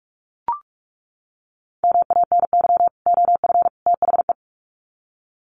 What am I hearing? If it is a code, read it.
Morse code, heard as a keyed tone, about 44 wpm, 700 Hz. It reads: MANY OF THE
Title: Sets of 3 Words 45wpm